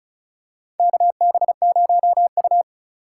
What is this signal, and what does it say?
Morse code, heard as a keyed tone, about 34 wpm, 700 Hz. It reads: KB0U
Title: USA Call Signs - 4 Characters - 3x Character Spacing - 35wpm